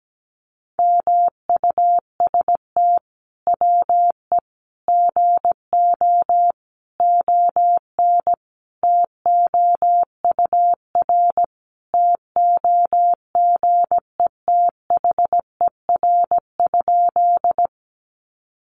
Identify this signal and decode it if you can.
Morse code, heard as a keyed tone, about 17 wpm, 700 Hz. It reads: MUST WE GO ON TOUR TOGETHER?
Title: Sentences from Top 1400 Words 17wpm